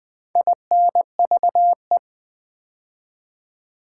Morse code, heard as a keyed tone, about 20 wpm, 700 Hz. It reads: INVE
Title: Build A Word - 4 Letters At A Time - 12-Letter Words - 20wpm